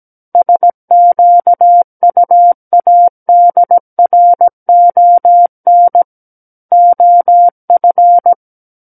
Morse code, heard as a keyed tone, about 17 wpm, 700 Hz. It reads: SQUADRON OF
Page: SKCC Learning Center - Morse Code Practice Files